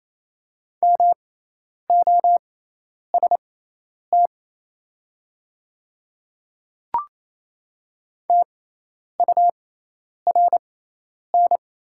Morse code, heard as a keyed tone, about 27 wpm, spaced as for 10 wpm, 700 Hz. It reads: MOST TURN
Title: 6x Character Spacing - Top 200 Words - 28wpm